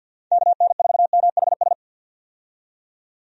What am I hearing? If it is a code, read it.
Morse code, heard as a keyed tone, about 47 wpm, 700 Hz. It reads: KN4MHS
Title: USA Call Signs - 6 Characters - 50wpm